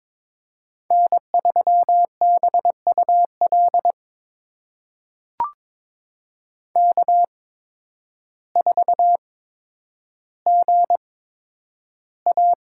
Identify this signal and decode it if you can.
Morse code, heard as a keyed tone, about 22 wpm, spaced as for 10 wpm, 700 Hz. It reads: N3BUL K4GA
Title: USA Call Signs - 5 Characters - 8x Character Spacing - 22wpm